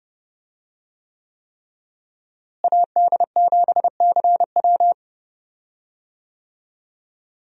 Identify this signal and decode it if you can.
Morse code, heard as a keyed tone, about 30 wpm, 700 Hz. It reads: AD7CW
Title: USA Call Signs - 5 Characters - Repeated 3x - 30wpm